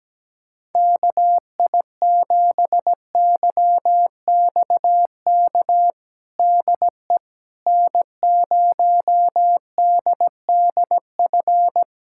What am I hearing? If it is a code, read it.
Morse code, heard as a keyed tone, about 17 wpm, 700 Hz. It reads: KI7YXK DE N0DDF
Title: USA Callsign DE Callsign 17wpm